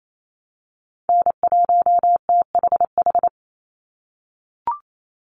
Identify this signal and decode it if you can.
Morse code, heard as a keyed tone, about 28 wpm, 700 Hz. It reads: N1THH
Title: USA Call Signs 28wpm